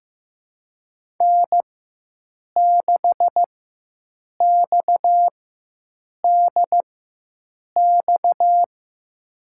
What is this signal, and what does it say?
Morse code, heard as a keyed tone, about 16 wpm, spaced as for 7 wpm, 700 Hz. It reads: N6XDX
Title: USA Call Signs - 5 Characters - 4x Character Spacing - 15wpm